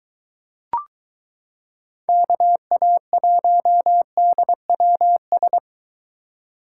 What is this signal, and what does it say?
Morse code, heard as a keyed tone, about 22 wpm, 700 Hz. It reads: KA1DWS